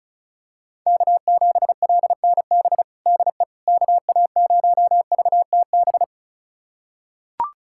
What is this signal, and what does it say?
Morse code, heard as a keyed tone, about 36 wpm, 700 Hz. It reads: K7LNB DE KA0VTB
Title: USA Callsign DE Callsign 35wpm